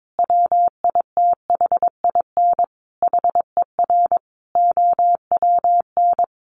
Morse code, heard as a keyed tone, about 22 wpm, 700 Hz. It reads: WITHIN HER OWN
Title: Sentences from Top 1200 Words 22wpm